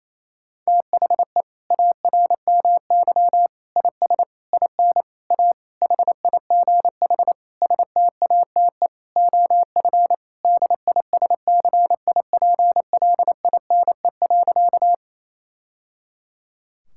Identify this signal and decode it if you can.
Morse code, heard as a keyed tone, about 28 wpm, 700 Hz. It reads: THE ARMY IS IN A HIGH STATE OF DISCIPLINE.
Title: Sentences from Top 1000 Words - Review Part 6 - 28wpm